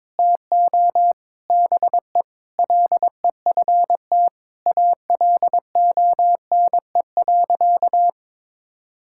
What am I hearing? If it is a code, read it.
Morse code, heard as a keyed tone, about 22 wpm, 700 Hz. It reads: TO BE LEFT ALONE.
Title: Sentences from Top 1000 Words - Review Part 6 - 22wpm